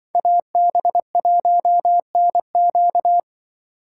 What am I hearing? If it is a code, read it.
Morse code, heard as a keyed tone, about 25 wpm, 700 Hz. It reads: AB1NQ